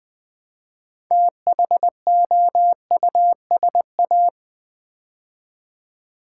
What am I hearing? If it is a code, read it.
Morse code, heard as a keyed tone, about 20 wpm, 700 Hz. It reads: THOUSA
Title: Build A Word - 3 Letters At A Time - 8-Letter Words - 20wpm